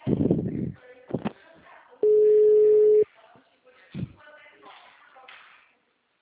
Ringback tone (Europe, 425 Hz).
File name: out- -2017- - - .18824